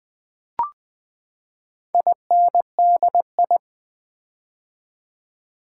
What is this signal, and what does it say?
Morse code, heard as a keyed tone, about 20 wpm, 700 Hz. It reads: INDI
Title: Build A Word - 4 Letters At A Time - 10-Letter Words - 20wpm